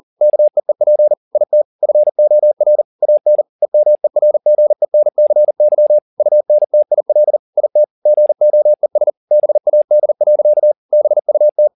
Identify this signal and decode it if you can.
Morse code, heard as a keyed tone, about 40 wpm, 600 Hz. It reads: KEEP IT FOR AN EMERGENCY UNTIL IT GOES BAD. BUT
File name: the old man and the sea 2 40wpm